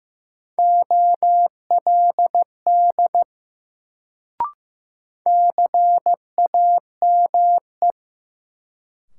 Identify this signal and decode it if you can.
Morse code, heard as a keyed tone, about 15 wpm, 700 Hz. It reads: OLD CAME
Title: Sentences from Top 200 Words - New Word Drill - 15wpm